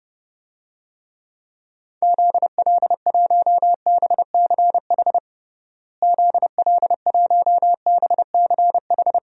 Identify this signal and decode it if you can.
Morse code, heard as a keyed tone, about 30 wpm, 700 Hz. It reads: ZL1BCH ZL1BCH
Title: DXCC-100 Call Signs - 6 Characters - Repeated 2x - 30wpm